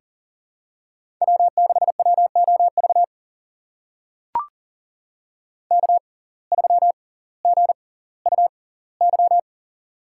Morse code, heard as a keyed tone, about 40 wpm, spaced as for 20 wpm, 700 Hz. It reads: W6WOV K3GUY
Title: USA Call Signs - 5 Characters - 6x Character Spacing - 40wpm